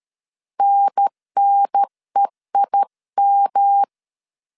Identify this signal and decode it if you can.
Morse code, heard as a keyed tone, about 13 wpm, 800 Hz. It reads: NNEIM
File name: CW / DET / 1 6 1500C S A 12wpm 3440